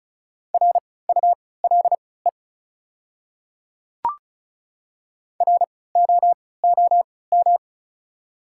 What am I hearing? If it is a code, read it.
Morse code, heard as a keyed tone, about 35 wpm, spaced as for 20 wpm, 700 Hz. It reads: RULE ROOM
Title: 3x Character Spacing - Top 500 Words - 35wpm